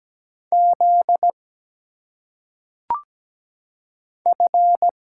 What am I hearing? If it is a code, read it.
Morse code, heard as a keyed tone, about 17 wpm, 700 Hz. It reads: Z F
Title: Single Letters - Phonetic Pronunciation 17wpm